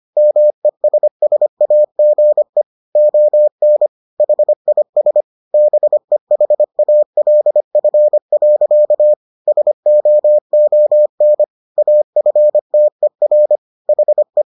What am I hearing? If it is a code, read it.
Morse code, heard as a keyed tone, about 25 wpm, 600 Hz. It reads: MESSAGE ON HIS BEHALF. SOON AFTER HE